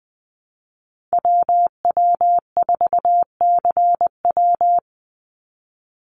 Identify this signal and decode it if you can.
Morse code, heard as a keyed tone, about 20 wpm, 700 Hz. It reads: WW4CW
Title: USA Call Signs 20wpm